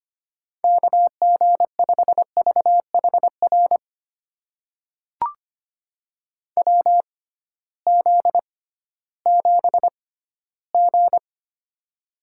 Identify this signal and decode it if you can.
Morse code, heard as a keyed tone, about 25 wpm, spaced as for 15 wpm, 700 Hz. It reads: KG5VHR WZ7G
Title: USA Call Signs - 6x Character Spacing - 25wpm